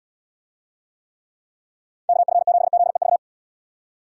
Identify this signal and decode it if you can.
Morse code, heard as a keyed tone, about 77 wpm, 700 Hz. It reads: KF7ZEV